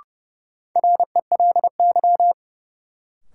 Morse code, heard as a keyed tone, about 30 wpm, 700 Hz. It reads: RELY
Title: Sentences from Top 1200 Words - New Word Drill - 30wpm